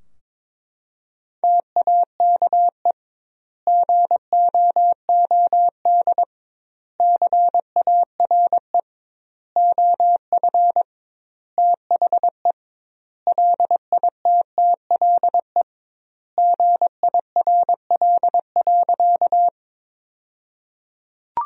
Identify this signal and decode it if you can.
Morse code, heard as a keyed tone, about 21 wpm, 700 Hz. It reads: TAKE GOOD CARE OF THE LITTLE GIRL.
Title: 2x Word-Spacing - Sentences from Top 400 Words - 22wpm